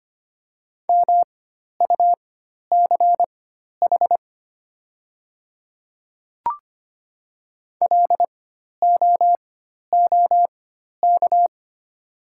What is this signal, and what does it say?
Morse code, heard as a keyed tone, about 25 wpm, spaced as for 12 wpm, 700 Hz. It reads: MUCH LOOK
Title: 4x Character Spacing - Top 200 Words - 25wpm